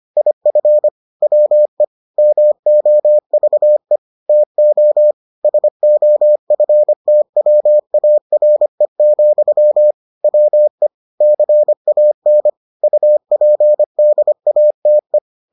Morse code, heard as a keyed tone, about 25 wpm, 600 Hz. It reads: IF WE MOVE TO SOFTWARE, WE CAN UPDATE